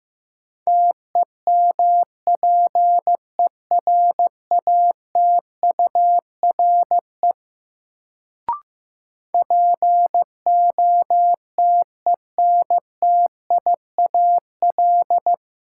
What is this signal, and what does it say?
Morse code, heard as a keyed tone, about 15 wpm, 700 Hz. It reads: TEMPERATURE POTENTIAL
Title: Sentences from Top 1000 Words - Review Part 4 - New Word Drill - 15wpm